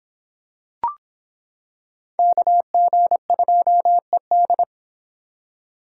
Morse code, heard as a keyed tone, about 26 wpm, 700 Hz. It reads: KG2ED